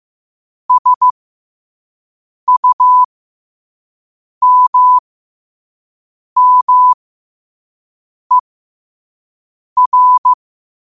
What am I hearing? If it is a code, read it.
Morse code, heard as a keyed tone, about 15 wpm, spaced as for 5 wpm, 1000 Hz. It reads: SUMMER
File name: qotdfiles / 5wpm